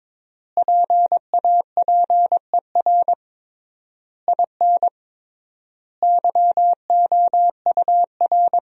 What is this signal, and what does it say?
Morse code, heard as a keyed tone, about 22 wpm, 700 Hz. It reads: PAPER IN YOUR
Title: 3x Word-Spacing - Sentences from Top 400 Words - 22wpm